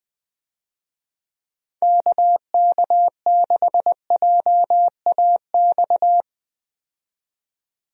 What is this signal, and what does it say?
Morse code, heard as a keyed tone, about 20 wpm, 700 Hz. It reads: KK6JAX
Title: USA Call Signs - 6 Characters - 20wpm